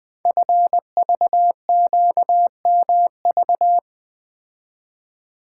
Morse code, heard as a keyed tone, about 20 wpm, 700 Hz. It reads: FVQMV
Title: Code Groups - 5 Letter:Numbers - 20wpm